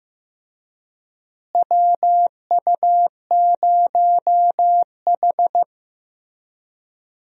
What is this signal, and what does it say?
Morse code, heard as a keyed tone, about 16 wpm, 700 Hz. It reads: WU0H
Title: USA Call Signs - 4 Characters - Repeated 2x - 15wpm